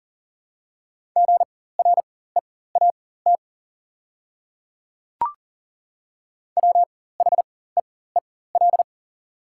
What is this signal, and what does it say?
Morse code, heard as a keyed tone, about 38 wpm, spaced as for 18 wpm, 690 Hz. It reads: GREAT WHEEL
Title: 4x Character Spacing - Top 500 Words - 40wpm